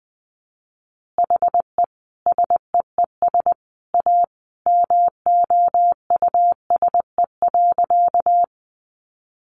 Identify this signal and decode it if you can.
Morse code, heard as a keyed tone, about 20 wpm, 700 Hz. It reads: HE SEES A MOUSE.
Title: Easy Phrases 20wpm